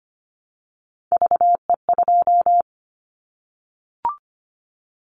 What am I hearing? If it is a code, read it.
Morse code, heard as a keyed tone, about 25 wpm, 700 Hz. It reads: VE2
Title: Callsign Prefixes for Top 25 DXCC - 25wpm